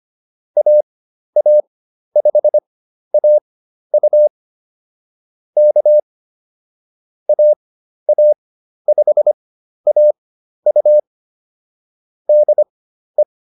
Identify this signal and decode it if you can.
Morse code, heard as a keyed tone, about 25 wpm, spaced as for 12 wpm, 600 Hz. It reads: AA5AU K AA5AU DE